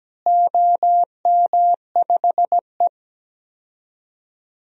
Morse code, heard as a keyed tone, about 17 wpm, 700 Hz. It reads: OM5E
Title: DXCC-100 Call Signs - 4 Characters - Repeated 2x - 17wpm